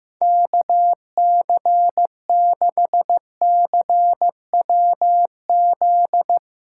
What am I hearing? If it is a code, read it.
Morse code, heard as a keyed tone, about 15 wpm, 700 Hz. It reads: KC6CWZ
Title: USA Callsign DE Callsign 15wpm